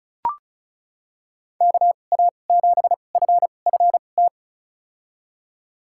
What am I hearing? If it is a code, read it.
Morse code, heard as a keyed tone, about 36 wpm, spaced as for 25 wpm, 690 Hz. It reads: KA7FFT
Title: USA Call Signs - 6 Characters - 2x Character Spacing - 35wpm